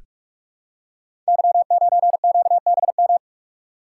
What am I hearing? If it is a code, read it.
Morse code, heard as a keyed tone, about 44 wpm, 700 Hz. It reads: Y9QBM
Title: Code Groups - 5 Letter:Numbers - 45wpm